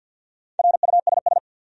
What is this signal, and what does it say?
Morse code, heard as a keyed tone, about 50 wpm, 700 Hz. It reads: RUSS
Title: USA Common Names - Max 5 Letters - 50wpm